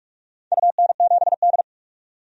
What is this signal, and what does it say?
Morse code, heard as a keyed tone, about 47 wpm, 700 Hz. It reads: UN7D
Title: DXCC-100 Call Signs - 4 Characters - 45wpm